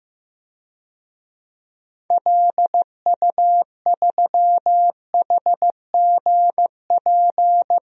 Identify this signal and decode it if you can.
Morse code, heard as a keyed tone, about 15 wpm, 700 Hz. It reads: LU3HGP
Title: DXCC-100 Call Signs - 6 Characters - 15wpm